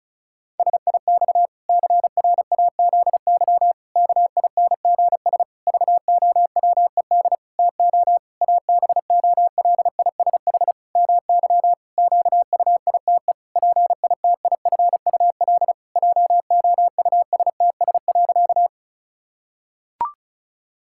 Morse code, heard as a keyed tone, about 36 wpm, 700 Hz. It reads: SIX CRAZY KINGS VOWED TO ABOLISH MY QUITE PITIFUL JOUSTS.
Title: Pangram Challenge v2 - 35wpm